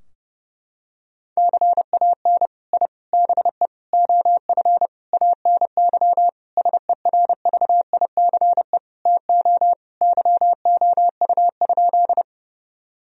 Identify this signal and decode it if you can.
Morse code, heard as a keyed tone, about 30 wpm, 700 Hz. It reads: CAN I BE OF ANY SERVICE TO YOU?
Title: Sentences from Top 1000 Words - Review Part 4 - 30wpm